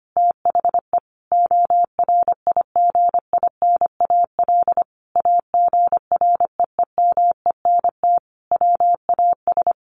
Morse code, heard as a keyed tone, about 25 wpm, 700 Hz. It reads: THE ORIGINAL AGREEMENT WAS
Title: Sentences from Top 1500 Words 25wpm